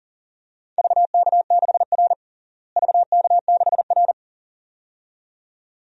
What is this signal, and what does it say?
Morse code, heard as a keyed tone, about 39 wpm, 700 Hz. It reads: VK6R VK6R
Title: DXCC-100 Call Signs - 4 Characters - Repeated 2x - 40wpm